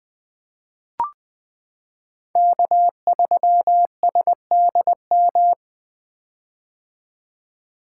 Morse code, heard as a keyed tone, about 20 wpm, 700 Hz. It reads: K3SDM
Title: 2023 POTA Hunter Call Signs - 20wpm